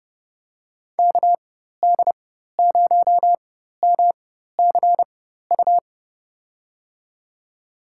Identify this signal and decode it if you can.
Morse code, heard as a keyed tone, about 31 wpm, spaced as for 14 wpm, 700 Hz. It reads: KD0MCU
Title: USA Call Signs - 6 Characters - 4x Character Spacing - 30wpm